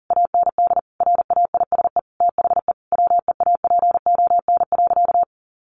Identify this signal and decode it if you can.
Morse code, heard as a keyed tone, about 40 wpm, 700 Hz. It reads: AND RAISE THE WEAPON.
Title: A Sound Of Thunder 40wpm